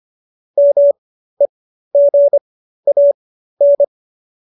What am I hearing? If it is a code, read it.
Morse code, heard as a keyed tone, about 25 wpm, spaced as for 13 wpm, 570 Hz. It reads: MEGAN